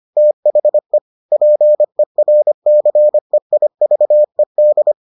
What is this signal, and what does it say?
Morse code, heard as a keyed tone, about 25 wpm, 600 Hz. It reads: THE PERCEIVED